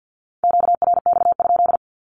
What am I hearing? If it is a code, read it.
Morse code, heard as a keyed tone, about 48 wpm, 700 Hz. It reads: QRX?
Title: Q-Codes 50wpm